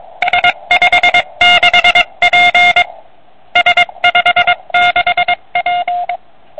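Morse code, heard as a keyed tone, about 22 wpm, 720 Hz, with noise. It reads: S56P S56P
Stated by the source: CW from a QRP QCX kit transceiver on 80 m, 5 W.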